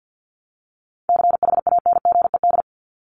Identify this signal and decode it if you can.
Morse code, heard as a keyed tone, about 50 wpm, 700 Hz. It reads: CHANGED
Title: ED Suffix - Top 100 Words with Base Words 50wpm